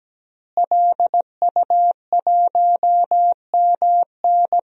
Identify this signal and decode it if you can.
Morse code, heard as a keyed tone, about 17 wpm, 700 Hz. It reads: LU1MN
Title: DXCC-100 Call Signs - 5 Characters - Repeated 2x - 17wpm